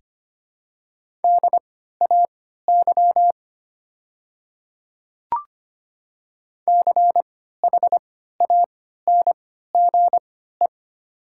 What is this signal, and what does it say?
Morse code, heard as a keyed tone, about 25 wpm, spaced as for 14 wpm, 700 Hz. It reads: DAY CHANGE
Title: 3x Character Spacing - Top 200 Words - 25wpm